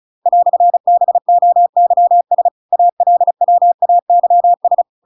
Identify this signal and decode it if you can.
Morse code, heard as a keyed tone, about 34 wpm, 700 Hz. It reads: "BOYS ALWAYS
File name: Pinocchio / Pinocchio ch8 35 35 ik1hge